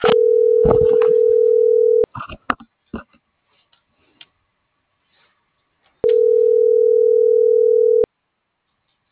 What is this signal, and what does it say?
Ringback tone (North America), 440 + 480 Hz.